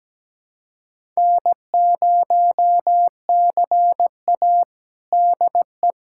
Morse code, heard as a keyed tone, about 17 wpm, 700 Hz. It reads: N0CA DE
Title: USA Callsign DE Callsign - 4 Characters - 17wpm